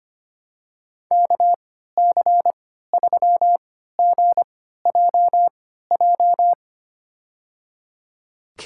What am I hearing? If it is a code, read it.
Morse code, heard as a keyed tone, about 25 wpm, spaced as for 14 wpm, 700 Hz. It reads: KC3GJJ
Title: USA Call Signs - 6 Characters - 3x Character Spacing - 25wpm